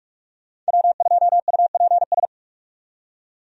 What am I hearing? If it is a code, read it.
Morse code, heard as a keyed tone, about 43 wpm, 700 Hz. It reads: W2UPS
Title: USA Call Signs - 5 Characters - 45wpm